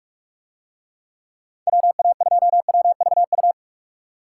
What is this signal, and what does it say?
Morse code, heard as a keyed tone, about 44 wpm, 700 Hz. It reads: WA2WVU